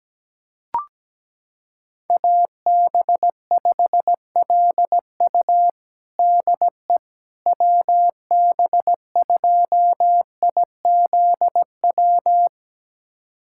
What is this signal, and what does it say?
Morse code, heard as a keyed tone, about 17 wpm, 700 Hz. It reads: AB5LU DE WB2IZW
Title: USA Callsign DE Callsign 17wpm